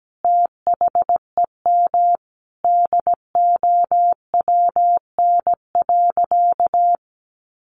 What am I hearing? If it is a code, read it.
Morse code, heard as a keyed tone, about 17 wpm, 700 Hz. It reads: THEM DOWN.
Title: Additional Easy Phrases 17wpm